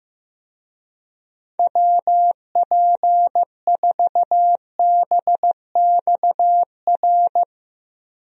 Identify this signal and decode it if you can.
Morse code, heard as a keyed tone, about 15 wpm, 700 Hz. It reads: WP4BXR
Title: USA Call Signs - 6 Characters - Repeated 2x - 15wpm